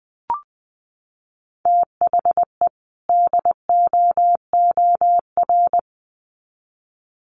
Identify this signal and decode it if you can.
Morse code, heard as a keyed tone, about 20 wpm, 700 Hz. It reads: THE DOOR
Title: Sets of 2 Words 20wpm